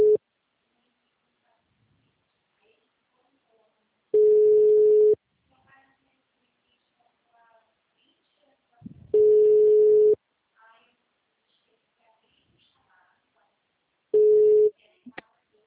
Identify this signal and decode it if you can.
Ringback tone (Europe, 425 Hz), with speech.
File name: out- -2002- - - .11